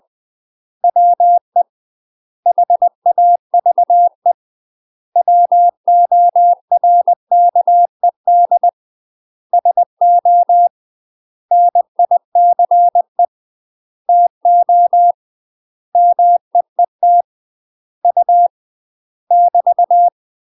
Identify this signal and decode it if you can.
Morse code, heard as a keyed tone, about 20 wpm, 700 Hz. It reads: WE HAVE WORKED SO NICE TO MEET U <BT>
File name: UK MASTER QSOs TRAINING / 20WPM / UK MASTER QSOs 0031